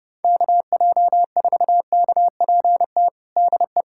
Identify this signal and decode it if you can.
Morse code, heard as a keyed tone, about 30 wpm, 700 Hz. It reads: KJ4KPT DE
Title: USA Callsign DE Callsign - 6 Characters - 30wpm